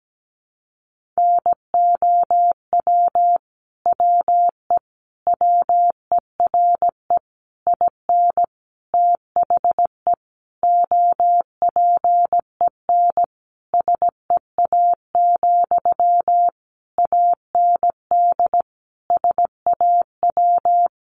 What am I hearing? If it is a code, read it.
Morse code, heard as a keyed tone, about 17 wpm, 700 Hz. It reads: NOW WE WERE IN THE OPEN SEA, AND SAW
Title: Sentences from Top 1200 Words 17wpm